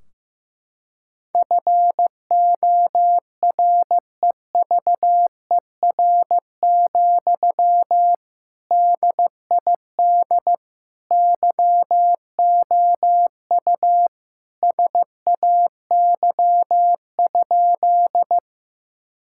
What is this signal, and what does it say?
Morse code, heard as a keyed tone, about 16 wpm, 700 Hz. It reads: FOREVER, DID YOU SAY?
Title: Sentences from Top 1600 Words 15wpm